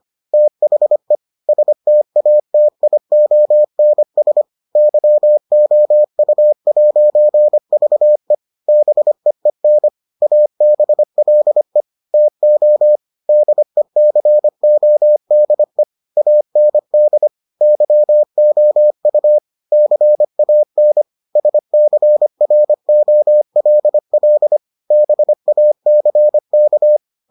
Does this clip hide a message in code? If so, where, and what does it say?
Yes: Morse code from 0.3 to 27.0 s, heard as a keyed tone, about 25 wpm, 600 Hz. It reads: THE STATIONS YOU'VE BEEN ABLE TO DECODE AND YOU CAN SCROLL BACK